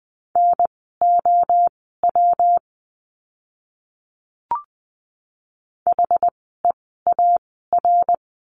Morse code, heard as a keyed tone, about 20 wpm, spaced as for 14 wpm, 700 Hz. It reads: NOW HEAR
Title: 206 Lesson 33f - Letter j - Words 20wpm